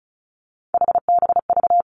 Morse code, heard as a keyed tone, about 36 wpm, 700 Hz. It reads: HBV
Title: Three Challenging Letter-Number Combinations 35wpm